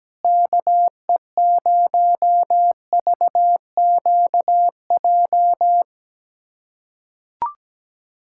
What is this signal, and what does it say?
Morse code, heard as a keyed tone, about 17 wpm, 700 Hz. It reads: KE0VQJ
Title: USA Callsign DE Callsign 17wpm